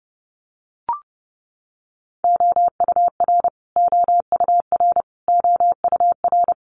Morse code, heard as a keyed tone, about 30 wpm, 700 Hz. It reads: OUR OUR OUR
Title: Top 500 Words Repeated 30wpm